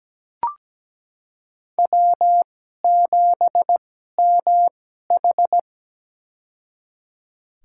Morse code, heard as a keyed tone, about 17 wpm, spaced as for 12 wpm, 700 Hz. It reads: W7MH
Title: USA Call Signs - 4 Characters - 2x Character Spacing - 17wpm